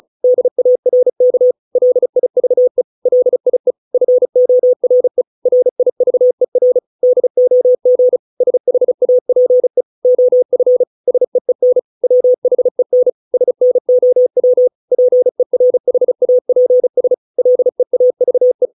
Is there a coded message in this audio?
Yes: Morse code from 0.2 to 18.7 s, heard as a keyed tone, about 34 wpm, 500 Hz. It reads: DARK LIVE LIE FORE RIVER DOG SHAPE OF SEEN WHEN SNOW PERHAPS LEAVE